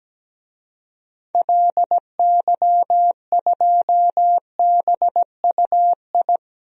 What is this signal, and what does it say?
Morse code, heard as a keyed tone, about 17 wpm, 700 Hz. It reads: LY2BUI
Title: DXCC-100 Call Signs - 6 Characters - 17wpm